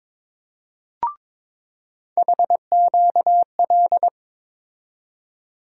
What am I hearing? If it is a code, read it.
Morse code, heard as a keyed tone, about 23 wpm, 700 Hz. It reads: HQL
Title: USA Call Sign Postfixes - 3 Letters 22wpm